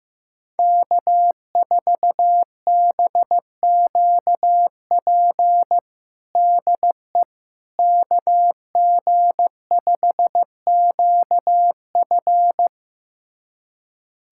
Morse code, heard as a keyed tone, about 16 wpm, 700 Hz. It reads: K4BQP DE KG5QF
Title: USA Callsign DE Callsign 15wpm